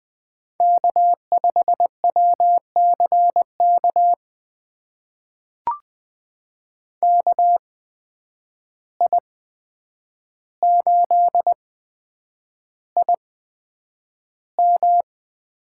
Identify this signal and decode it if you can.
Morse code, heard as a keyed tone, about 20 wpm, spaced as for 9 wpm, 700 Hz. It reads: K5WCK KI8IM
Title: USA Call Signs - 5 Characters - 8x Character Spacing - 20wpm